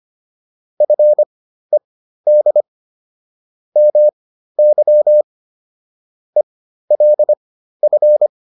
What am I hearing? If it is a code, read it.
Morse code, heard as a keyed tone, about 25 wpm, spaced as for 13 wpm, 600 Hz. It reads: FED MY ELF
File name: PR202 13